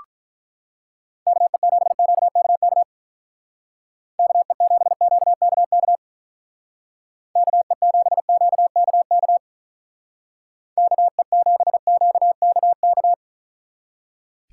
Morse code, heard as a keyed tone, about 43 wpm, 700 Hz. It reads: KE7QKK KE7QKK KE7QKK KE7QKK
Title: Speed Racing - USA 2x3 Call Signs 35wpm